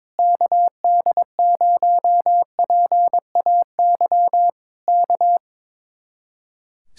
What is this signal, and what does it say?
Morse code, heard as a keyed tone, about 22 wpm, 700 Hz. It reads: KD0PAY K
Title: Speed-Racing Common QSO Phrases - 22wpm